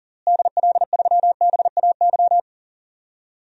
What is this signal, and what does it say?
Morse code, heard as a keyed tone, about 40 wpm, 700 Hz. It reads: DL3BAY
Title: DXCC-100 Call Signs - 6 Characters - Repeated 3x - 40wpm